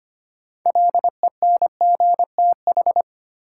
Morse code, heard as a keyed tone, about 25 wpm, 700 Hz. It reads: LENGTH
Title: Sentences from Top 1000 Words - New Word Drill - 25wpm